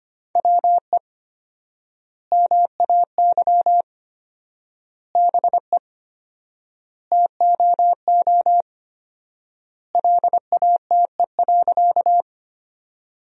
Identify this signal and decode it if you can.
Morse code, heard as a keyed tone, about 25 wpm, 700 Hz. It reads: WE MAY BE TOO LATE.
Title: 4x Word-Spacing - Sentences from Top 300 Words - 25wpm